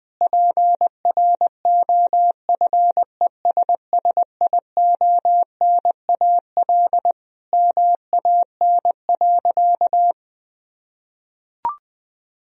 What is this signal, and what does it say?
Morse code, heard as a keyed tone, about 20 wpm, 700 Hz. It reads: PROFESSIONAL MAN.
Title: Sentences from Top 1000 Words - Review Part 4 - 20wpm